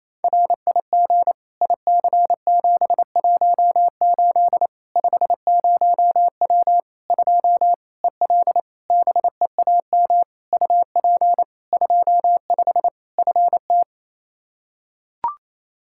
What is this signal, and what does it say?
Morse code, heard as a keyed tone, about 28 wpm, 700 Hz. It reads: RIG IC718 50W 2 EL BEAM UP 25 FT
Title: K1USN SST QSO Practice 28wpm